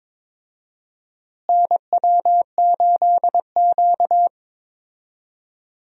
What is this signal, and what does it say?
Morse code, heard as a keyed tone, about 22 wpm, 700 Hz. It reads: NW8Q
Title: USA Call Signs - 4 Characters - Repeated 3x - 22wpm